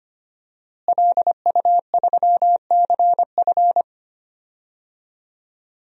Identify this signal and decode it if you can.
Morse code, heard as a keyed tone, about 25 wpm, 700 Hz. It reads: LU3CF
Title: DXCC-100 Call Signs - 5 Characters - Repeated 3x - 25wpm